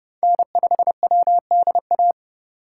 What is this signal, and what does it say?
Morse code, heard as a keyed tone, about 30 wpm, 700 Hz. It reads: N5WDA
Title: USA Call Signs - 5 Characters - 4x Character Spacing - 30wpm